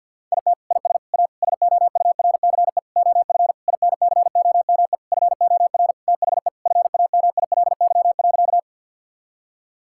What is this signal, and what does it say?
Morse code, heard as a keyed tone, about 50 wpm, 700 Hz. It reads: IT IS A SOURCE OF INCOME FOR THE FAMILY.
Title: Sentences from Top 1000 Words - Review Part 5 - 50wpm